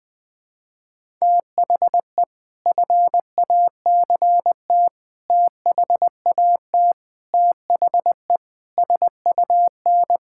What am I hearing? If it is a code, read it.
Morse code, heard as a keyed tone, about 20 wpm, 700 Hz. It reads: THE FACT THAT THE SUN
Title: Sentences with 4-Word Phrases - 20wpm